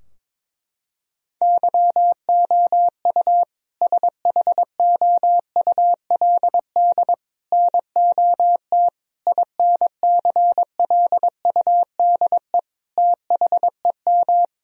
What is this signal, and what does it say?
Morse code, heard as a keyed tone, about 22 wpm, 700 Hz. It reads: YOU SHOULD NOT INCLUDE THEM
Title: Sentences from Top 1000 Words - Review Part 3 - 22wpm